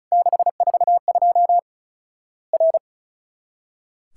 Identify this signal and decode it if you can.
Morse code, heard as a keyed tone, about 34 wpm, 700 Hz. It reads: 642 R
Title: Worked All Europe DX Contest - QTC Exchange - 35wpm